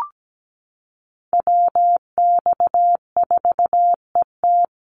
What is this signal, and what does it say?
Morse code, heard as a keyed tone, about 17 wpm, 700 Hz. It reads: WX4ET
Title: NA SOTA Chaser Callsigns 17wpm